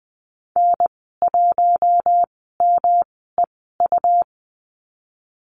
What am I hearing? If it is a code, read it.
Morse code, heard as a keyed tone, about 20 wpm, spaced as for 15 wpm, 700 Hz. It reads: N1MEU
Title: 122 Lesson 20g - Number 3 - Call Signs 20wpm